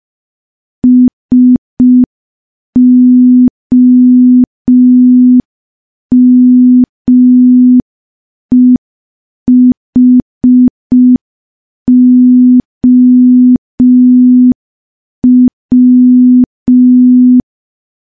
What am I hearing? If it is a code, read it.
Morse code, heard as a keyed tone, about 5 wpm, 250 Hz. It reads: SOMEHOW
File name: morse / morse-training-05WPM